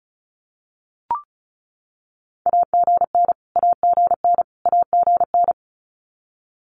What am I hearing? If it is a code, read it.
Morse code, heard as a keyed tone, about 34 wpm, 700 Hz. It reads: AGN AGN AGN
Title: Abbreviations Repeated 3x - 35wpm